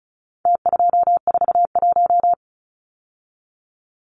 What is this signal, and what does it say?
Morse code, heard as a keyed tone, about 34 wpm, 700 Hz. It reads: T241
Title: Numbers - Contest Style - 4-Digit 35wpm